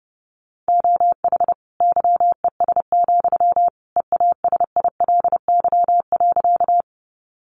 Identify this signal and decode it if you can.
Morse code, heard as a keyed tone, about 30 wpm, 700 Hz. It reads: OH YES, EASILY.
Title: Sentences from Top 900 Words 30wpm